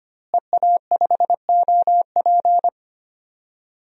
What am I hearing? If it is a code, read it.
Morse code, heard as a keyed tone, about 25 wpm, 700 Hz. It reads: EA5OP
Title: DXCC-100 Call Signs - 5 Characters - Repeated 2x - 25wpm